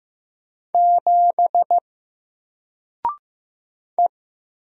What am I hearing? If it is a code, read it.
Morse code, heard as a keyed tone, about 15 wpm, 700 Hz. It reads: 7 E
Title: Single Character (Letter:Number:Punctuation:Prosign) 15wpm